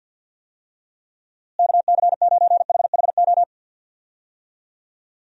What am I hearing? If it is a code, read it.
Morse code, heard as a keyed tone, about 51 wpm, 700 Hz. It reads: KC9HHO
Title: USA Call Signs - 6 Characters - Repeated 2x - 50wpm